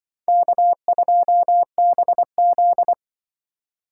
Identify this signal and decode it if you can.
Morse code, heard as a keyed tone, about 24 wpm, 700 Hz. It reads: K2BZ
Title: Speed Racing - USA Call Signs 20wpm